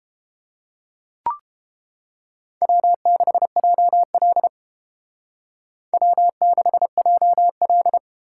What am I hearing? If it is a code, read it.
Morse code, heard as a keyed tone, about 29 wpm, 700 Hz. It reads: W6JL W6JL